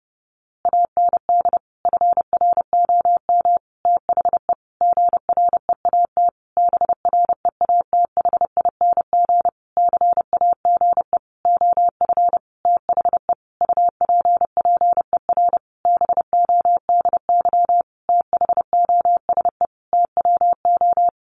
Morse code, heard as a keyed tone, about 30 wpm, 700 Hz. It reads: AND FROM THE GREAT BREATHING CAGE OF THE UPPER BODY THOSE TWO
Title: A Sound Of Thunder 30wpm